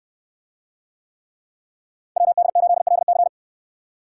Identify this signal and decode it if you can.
Morse code, heard as a keyed tone, about 68 wpm, 690 Hz. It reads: WD8LC